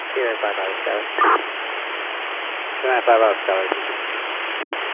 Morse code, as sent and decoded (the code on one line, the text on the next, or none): none